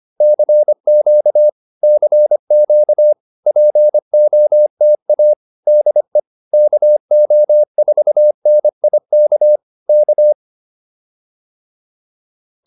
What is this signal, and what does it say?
Morse code, heard as a keyed tone, about 25 wpm, 590 Hz. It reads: CQ CQ POTA DE KO4NIK K